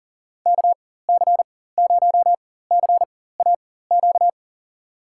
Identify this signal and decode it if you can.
Morse code, heard as a keyed tone, about 39 wpm, spaced as for 18 wpm, 700 Hz. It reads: KC0CAQ
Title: USA Call Signs - 6 Characters - 4x Character Spacing - 40wpm